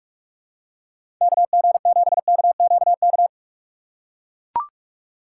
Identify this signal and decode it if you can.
Morse code, heard as a keyed tone, about 43 wpm, 700 Hz. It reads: KG7KQK
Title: USA Call Signs - 6 Characters - 2x Character Spacing - 45wpm